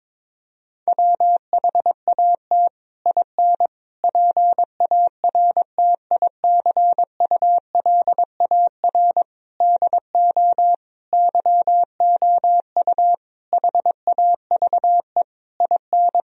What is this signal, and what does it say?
Morse code, heard as a keyed tone, about 21 wpm, 700 Hz. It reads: WHAT IN PARTICULAR DO YOU HAVE IN
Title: Sentences from Top 1000 Words - Review Part 4 - 22wpm